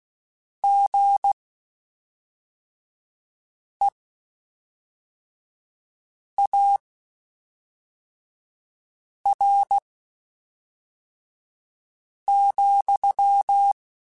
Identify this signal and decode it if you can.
Morse code, heard as a keyed tone, about 16 wpm, spaced as for 3 wpm, 800 Hz. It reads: GEAR,
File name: Spaced / Chuckles Spaced-mp3s / BrainMouth sp @5